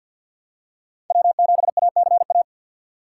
Morse code, heard as a keyed tone, about 47 wpm, 700 Hz. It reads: W7ACA
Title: USA Call Signs - 5 Characters - 2x Character Spacing - 50wpm